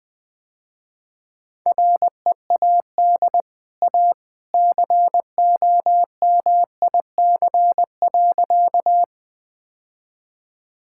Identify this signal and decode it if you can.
Morse code, heard as a keyed tone, about 20 wpm, 700 Hz. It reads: READ A COMIC.
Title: Easy Sentences with Hints - 20wpm